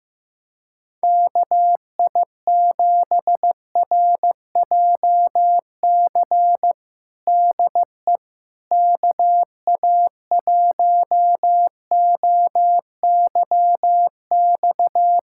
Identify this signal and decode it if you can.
Morse code, heard as a keyed tone, about 15 wpm, 700 Hz. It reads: KI7RJC DE KA1OYX
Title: USA Callsign DE Callsign - 6 Characters - 15wpm